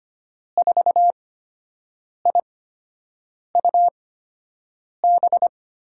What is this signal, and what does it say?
Morse code, heard as a keyed tone, about 25 wpm, spaced as for 7 wpm, 700 Hz. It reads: 4IUB
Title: USA Call Signs - 6 Characters - 8x Character Spacing - 25wpm